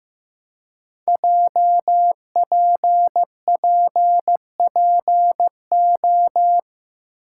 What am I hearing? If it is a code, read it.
Morse code, heard as a keyed tone, about 15 wpm, 700 Hz. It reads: JPPPO
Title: Code Groups - 5 Letters - 15wpm